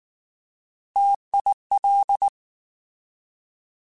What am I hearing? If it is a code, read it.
Morse code, heard as a keyed tone, about 19 wpm, 800 Hz. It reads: TIL
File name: Alphabetical / Alphabetical mp3s / Ts @18